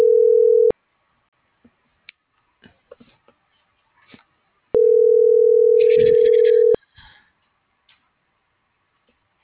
Ringback tone (North America), 440 + 480 Hz.